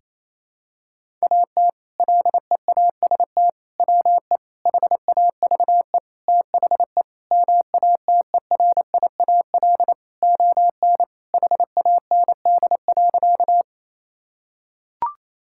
Morse code, heard as a keyed tone, about 28 wpm, 700 Hz. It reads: AT LEAST WE HAVE THE MATERIAL ON HAND.
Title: Sentences from Top 1000 Words - Review Part 5 - 28wpm